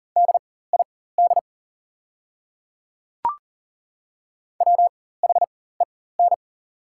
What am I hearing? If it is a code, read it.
Morse code, heard as a keyed tone, about 40 wpm, spaced as for 19 wpm, 690 Hz. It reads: DID WHEN
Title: 4x Character Spacing - Top 500 Words - 40wpm